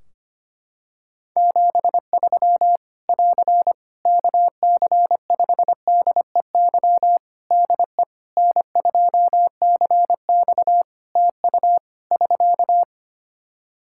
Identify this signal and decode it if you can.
Morse code, heard as a keyed tone, about 25 wpm, 700 Hz. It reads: 73 <AR> KC5DEY DE N2CX TU <SK>
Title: Practice QSO - Exchange 4 - Ending - 25wpm